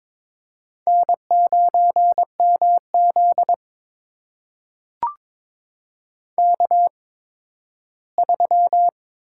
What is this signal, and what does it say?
Morse code, heard as a keyed tone, about 22 wpm, spaced as for 13 wpm, 700 Hz. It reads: N9MZ K3
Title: USA Call Signs - 4 Characters - 8x Character Spacing - 22wpm